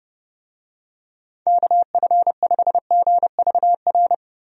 Morse code, heard as a keyed tone, about 29 wpm, 700 Hz. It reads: KF5GVR
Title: USA Call Signs - 6 Characters - Repeated 3x - 30wpm